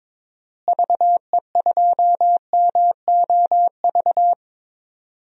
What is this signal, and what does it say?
Morse code, heard as a keyed tone, about 22 wpm, 700 Hz. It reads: VE2MOV
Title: DXCC-100 Call Signs - 6 Characters - Repeated 3x - 22wpm